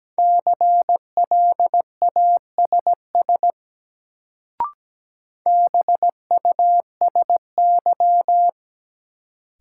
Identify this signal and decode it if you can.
Morse code, heard as a keyed tone, about 17 wpm, 700 Hz. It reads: CLASS BUSY
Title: Sentences from Top 1000 Words - Review Part 2 - New Word Drill - 17wpm